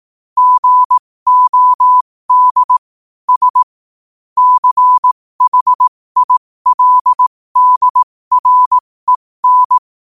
Morse code, heard as a keyed tone, about 18 wpm, spaced as for 15 wpm, 1000 Hz. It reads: GOD S CHILDREN
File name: qotdfiles / 15wpm